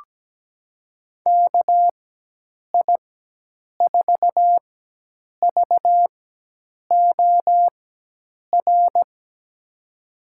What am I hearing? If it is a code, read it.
Morse code, heard as a keyed tone, about 17 wpm, spaced as for 8 wpm, 700 Hz. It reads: KI4VOR
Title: USA Call Signs - 6 Characters - 4x Character Spacing - 17wpm